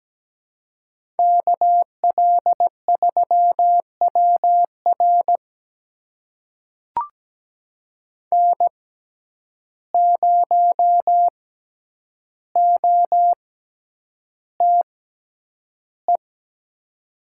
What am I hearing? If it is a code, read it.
Morse code, heard as a keyed tone, about 17 wpm, spaced as for 9 wpm, 700 Hz. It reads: KL3WR N0OTE
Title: USA Call Signs - 5 Characters - 6x Character Spacing - 17wpm